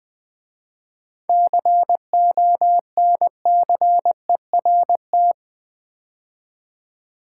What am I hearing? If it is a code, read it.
Morse code, heard as a keyed tone, about 20 wpm, 700 Hz. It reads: CONCERT
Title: 7 Letter Words - First 3 Letters Given - 20wpm